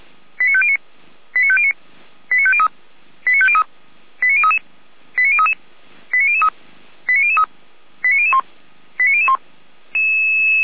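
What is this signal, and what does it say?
Five-tone selective call (ZVEI), heard as quick runs of five tones, tones of about 70 ms. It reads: 89509 89509 89503 89503 89033 89033 89003 89003 89001 89001
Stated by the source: Five-tone ZVEI selective call received on an SDR, from the feuerwehr-orlamuende.org fire brigade audio server.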